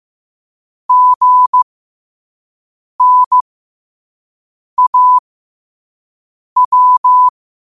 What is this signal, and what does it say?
Morse code, heard as a keyed tone, about 15 wpm, spaced as for 5 wpm, 1000 Hz. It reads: GNAW
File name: qotdfiles / 5wpm